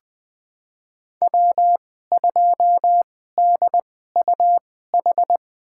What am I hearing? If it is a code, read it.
Morse code, heard as a keyed tone, about 20 wpm, spaced as for 15 wpm, 700 Hz. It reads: W2DUH
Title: USA Call Signs - 5 Characters - 2x Character Spacing - 20wpm